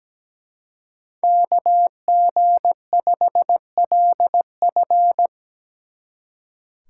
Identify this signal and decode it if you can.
Morse code, heard as a keyed tone, about 17 wpm, 700 Hz. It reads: KG5LF
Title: Speed Racing - USA 2x2 Call Signs 17wpm